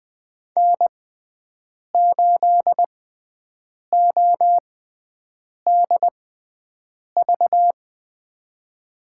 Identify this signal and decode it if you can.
Morse code, heard as a keyed tone, about 20 wpm, spaced as for 7 wpm, 700 Hz. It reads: N8ODV
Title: USA Call Signs - 5 Characters - 6x Character Spacing - 20wpm